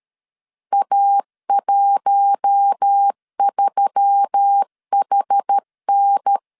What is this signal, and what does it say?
Morse code, heard as a keyed tone, about 13 wpm, 800 Hz. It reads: A13HN
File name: LEZIONI / 7 GF9Z / 7 GF9 12wpm 2319